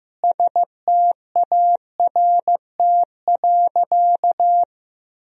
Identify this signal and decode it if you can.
Morse code, heard as a keyed tone, about 15 wpm, 700 Hz. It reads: START.
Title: Sentences from Top 1000 Words - Review Part 1 - 15wpm